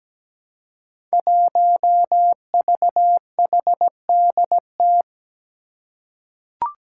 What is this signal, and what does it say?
Morse code, heard as a keyed tone, about 17 wpm, 700 Hz. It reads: 1VHDT
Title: Five Challenging Letter-Number Combinations 17wpm